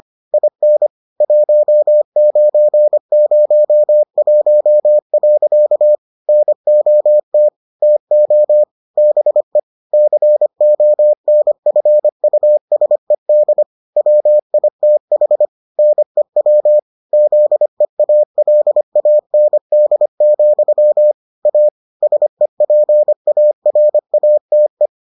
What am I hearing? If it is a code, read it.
Morse code, heard as a keyed tone, about 25 wpm, 600 Hz. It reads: IN 1901. NOT TO BE CONFUSED WITH NEW ZEALAND, A SEPARATE